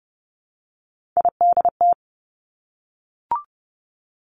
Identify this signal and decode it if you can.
Morse code, heard as a keyed tone, about 30 wpm, 700 Hz. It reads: IDT
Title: Three Challenging Letter-Number Combinations 30wpm